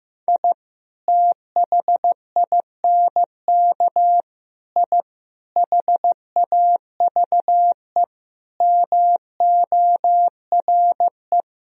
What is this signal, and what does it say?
Morse code, heard as a keyed tone, about 15 wpm, 700 Hz. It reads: I THINK I HAVE MORE
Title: Sentences from Top 1000 Words - Review Part 3 - 15wpm